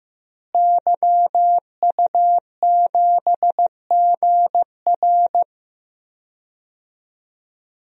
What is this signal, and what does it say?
Morse code, heard as a keyed tone, about 15 wpm, 700 Hz. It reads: YU7GR
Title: DXCC-100 Call Signs - 5 Characters - Repeated 2x - 15wpm